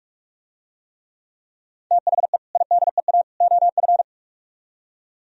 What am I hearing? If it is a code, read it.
Morse code, heard as a keyed tone, about 46 wpm, 700 Hz. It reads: THE IDEA OF